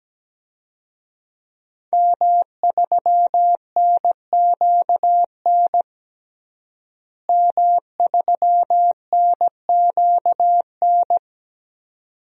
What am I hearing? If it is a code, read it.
Morse code, heard as a keyed tone, about 17 wpm, 700 Hz. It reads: M3NQN M3NQN
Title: DXCC-100 Call Signs - 5 Characters - Repeated 2x - 17wpm